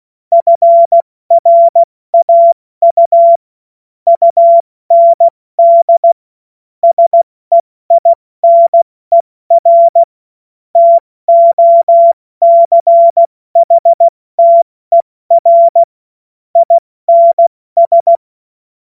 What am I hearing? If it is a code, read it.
Morse code, heard as a keyed tone, about 16 wpm, 680 Hz. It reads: FRAU UND SEINER TOCHTER INS
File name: Morsekurs / Witze / Witz 31. Januar 2023 70 F80